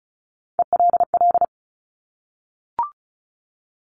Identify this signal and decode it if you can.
Morse code, heard as a keyed tone, about 36 wpm, 700 Hz. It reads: ELL
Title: Top Three Letter Combinations in Words 35wpm